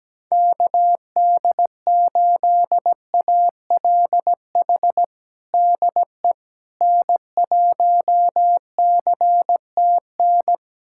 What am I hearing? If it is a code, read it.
Morse code, heard as a keyed tone, about 17 wpm, 700 Hz. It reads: KD8ALH DE N1CTN
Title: USA Callsign DE Callsign 17wpm